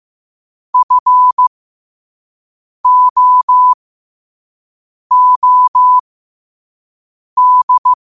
Morse code, heard as a keyed tone, about 15 wpm, spaced as for 5 wpm, 1000 Hz. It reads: FOOD
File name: qotdfiles / 5wpm